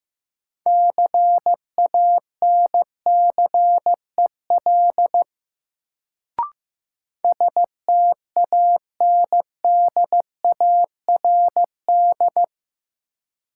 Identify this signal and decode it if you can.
Morse code, heard as a keyed tone, about 15 wpm, 700 Hz. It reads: CANCEL STANDARD
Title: Sentences from Top 1200 Words - New Word Drill Alt - 15wpm